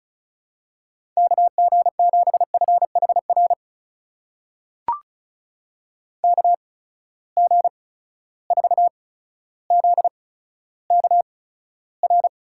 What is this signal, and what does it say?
Morse code, heard as a keyed tone, about 34 wpm, spaced as for 15 wpm, 700 Hz. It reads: KG7FHR KG4ZKR
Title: USA Call Signs - 6 Characters - 8x Character Spacing - 35wpm